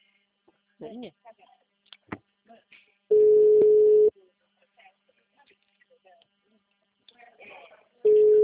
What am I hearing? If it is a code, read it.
Ringback tone (Europe, 425 Hz), with speech.
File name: out- -2005- - - .2177